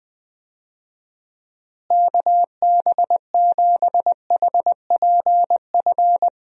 Morse code, heard as a keyed tone, about 20 wpm, 700 Hz. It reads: KB7HPF
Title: USA Call Signs - 6 Characters - Repeated 2x - 20wpm